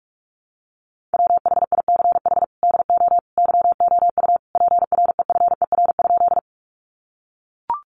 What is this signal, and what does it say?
Morse code, heard as a keyed tone, about 44 wpm, 700 Hz. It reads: WHICH DO YOU PREFER?
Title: Sentences from Top 1100 Words 45wpm